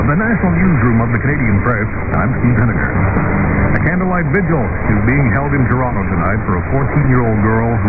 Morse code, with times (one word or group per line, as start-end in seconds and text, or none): none